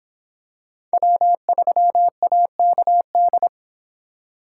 Morse code, heard as a keyed tone, about 23 wpm, 700 Hz. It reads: W3AKD